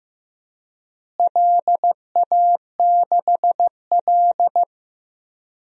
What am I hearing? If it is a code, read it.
Morse code, heard as a keyed tone, about 15 wpm, 700 Hz. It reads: LA6L
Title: DXCC-100 Call Signs - 4 Characters - Repeated 3x - 15wpm